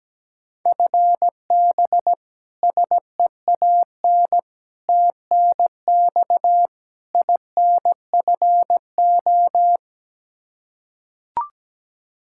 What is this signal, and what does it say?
Morse code, heard as a keyed tone, about 17 wpm, 700 Hz. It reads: FB SEAN TNX INFO
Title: Practice QSO - Exchange 3 - More Info - 17wpm